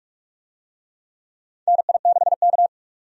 Morse code, heard as a keyed tone, about 45 wpm, 700 Hz. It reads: NI6K
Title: USA Call Signs - 4 Characters - 45wpm